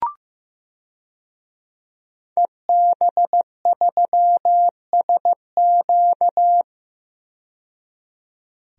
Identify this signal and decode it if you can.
Morse code, heard as a keyed tone, about 15 wpm, 700 Hz. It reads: EB3SQ
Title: DXCC-100 Call Signs - 5 Characters - 15wpm